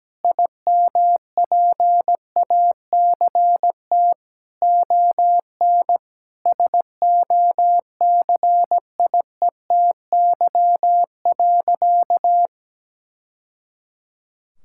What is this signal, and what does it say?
Morse code, heard as a keyed tone, about 17 wpm, 700 Hz. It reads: IMPACT ON SOCIETY.
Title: Sentences from Top 1000 Words - Review Part 6 - 17wpm